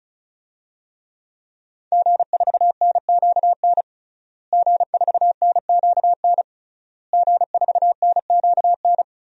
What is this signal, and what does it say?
Morse code, heard as a keyed tone, about 36 wpm, 700 Hz. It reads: G4NQN G4NQN G4NQN
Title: DXCC-100 Call Signs - 5 Characters - Repeated 3x - 35wpm